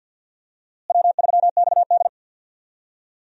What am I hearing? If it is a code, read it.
Morse code, heard as a keyed tone, about 46 wpm, 700 Hz. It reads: W3XD
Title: USA Call Signs - 4 Characters - Repeated 3x - 50wpm